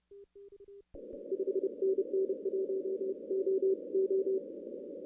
Morse code, heard as a keyed tone, about 32 wpm, spaced as for 27 wpm, 380 Hz. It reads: 5NN 1OO
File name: 2016 arrl dx cw / 2016-ARRL-DX-CW-K5TR- - 0-26